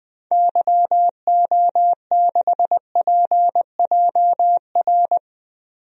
Morse code, heard as a keyed tone, about 20 wpm, 700 Hz. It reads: YO6PJR
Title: DXCC-100 Call Signs - 6 Characters - 20wpm